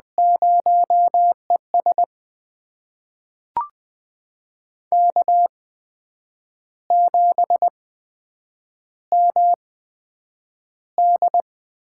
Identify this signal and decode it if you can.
Morse code, heard as a keyed tone, about 20 wpm, spaced as for 8 wpm, 700 Hz. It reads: 0ES K7MD
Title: USA Call Signs - 8x Character Spacing - 20wpm